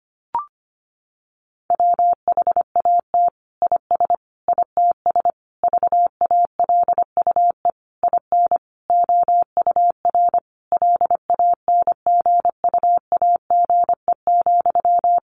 Morse code, heard as a keyed tone, about 25 wpm, 700 Hz. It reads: WHAT IS ITS VALUE IN OUR LANGUAGE,
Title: Sentences from Top 600 Words 25wpm